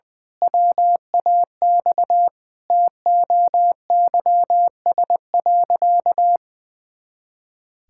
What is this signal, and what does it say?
Morse code, heard as a keyed tone, about 21 wpm, 700 Hz. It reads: WAX TOYS.
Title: Pangram Challenge v2 - 20wpm